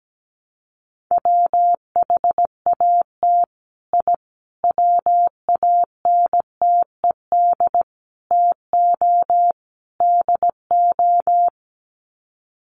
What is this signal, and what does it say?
Morse code, heard as a keyed tone, about 17 wpm, 700 Hz. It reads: WHAT I WANTED TO DO
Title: Sets of 5 Words Encore2 17wpm